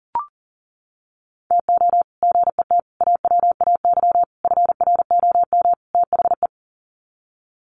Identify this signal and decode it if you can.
Morse code, heard as a keyed tone, about 40 wpm, 700 Hz. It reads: TO GET AWAY FROM THE
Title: Sets of 5 Words Encore2 40wpm